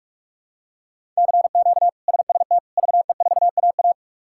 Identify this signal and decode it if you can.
Morse code, heard as a keyed tone, about 46 wpm, 700 Hz. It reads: CQ SST VE4AA
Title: K1USN SST QSO Practice - 45wpm